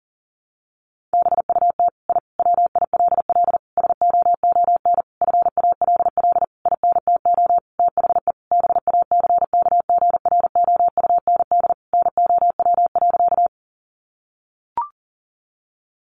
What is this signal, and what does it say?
Morse code, heard as a keyed tone, about 37 wpm, 700 Hz. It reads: BUT I WILL SOON FALL INTO THE BACKGROUND NOW.
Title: Sentences from Top 1300 Words 40wpm